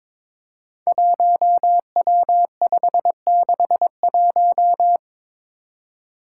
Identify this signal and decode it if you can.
Morse code, heard as a keyed tone, about 22 wpm, 700 Hz. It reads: 1W561
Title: Five Challenging Letter-Number Combinations 22wpm